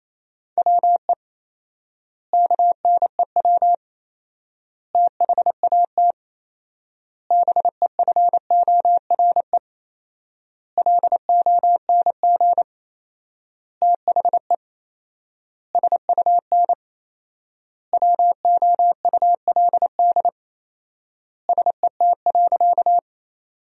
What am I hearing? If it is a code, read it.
Morse code, heard as a keyed tone, about 27 wpm, 700 Hz. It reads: WE KNEW THAT BEFORE LONG THE SUN WOULD SET.
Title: 4x Word-Spacing - Sentences from Top 400 Words - 28wpm